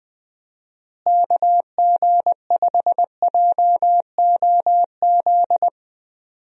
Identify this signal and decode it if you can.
Morse code, heard as a keyed tone, about 20 wpm, 700 Hz. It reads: KG5JOZ
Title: USA Call Signs - 6 Characters - Repeated 2x - 20wpm